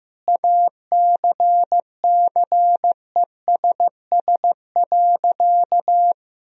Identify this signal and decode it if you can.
Morse code, heard as a keyed tone, about 15 wpm, 700 Hz. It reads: ACCESS.
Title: Sentences from Top 1000 Words - Review Part 4 - 15wpm